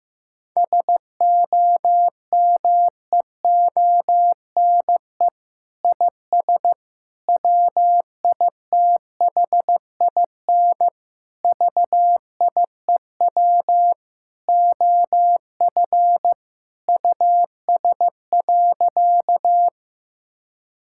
Morse code, heard as a keyed tone, about 15 wpm, 700 Hz. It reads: SOMEONE IS WITHIN VIEW OF US.
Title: Sentences from Top 1000 Words - Review Part 6 - 15wpm